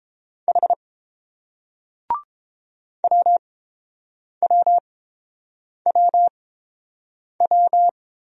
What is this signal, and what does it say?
Morse code, heard as a keyed tone, about 30 wpm, spaced as for 7 wpm, 690 Hz. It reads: H WWWW